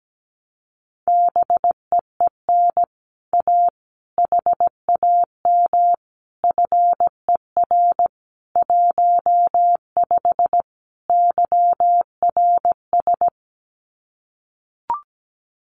Morse code, heard as a keyed tone, about 17 wpm, 700 Hz. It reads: BEEN A HAM FER 15 YRS
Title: Common QSO Phrases Encore 17wpm